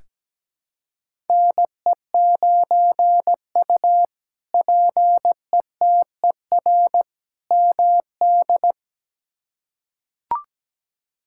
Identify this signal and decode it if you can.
Morse code, heard as a keyed tone, about 17 wpm, 700 Hz. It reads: NE9U PETER MD
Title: K1USN SST QSO Practice - 17wpm